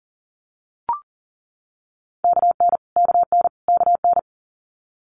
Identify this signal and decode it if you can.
Morse code, heard as a keyed tone, about 38 wpm, 700 Hz. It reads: KN KN KN
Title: Multi-letter Phonemes Repeated 3x 40wpm